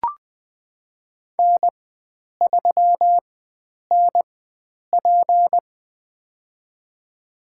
Morse code, heard as a keyed tone, about 20 wpm, spaced as for 9 wpm, 700 Hz. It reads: N3NP
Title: USA Call Signs - 4 Characters - 4x Character Spacing - 20wpm